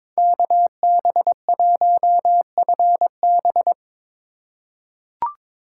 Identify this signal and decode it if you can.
Morse code, heard as a keyed tone, about 21 wpm, 700 Hz. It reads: KB1FB
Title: USA Call Signs - 5 Characters - 2x Character Spacing - 22wpm